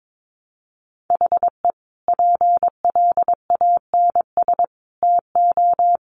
Morse code, heard as a keyed tone, about 22 wpm, 700 Hz. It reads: HE PLANS TO
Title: Additional Easy Phrases 22wpm